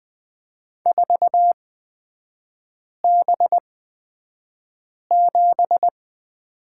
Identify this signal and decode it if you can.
Morse code, heard as a keyed tone, about 20 wpm, spaced as for 5 wpm, 700 Hz. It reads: 4B7
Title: 236 Lesson 38e - Letter q - Three Characters 20wpm